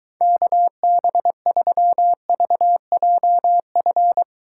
Morse code, heard as a keyed tone, about 23 wpm, 700 Hz. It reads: KB3VJF